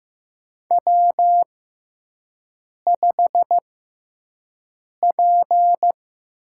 Morse code, heard as a keyed tone, about 15 wpm, spaced as for 5 wpm, 700 Hz. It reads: W5P
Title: USA Call Signs - 5 Characters - 6x Character Spacing - 15wpm